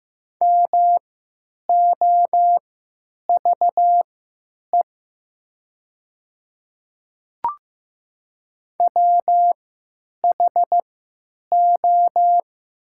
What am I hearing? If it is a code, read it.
Morse code, heard as a keyed tone, about 15 wpm, spaced as for 9 wpm, 700 Hz. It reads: MOVE WHO
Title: 3x Character Spacing - Top 200 Words - 15wpm